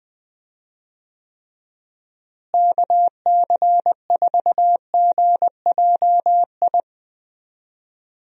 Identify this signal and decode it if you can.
Morse code, heard as a keyed tone, about 21 wpm, 700 Hz. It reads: KC4GJI
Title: USA Call Signs - 6 Characters - Repeated 2x - 20wpm